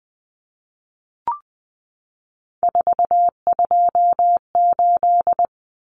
Morse code, heard as a keyed tone, about 20 wpm, 700 Hz. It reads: 428
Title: Common Cut Numbers - 3-Digit 20wpm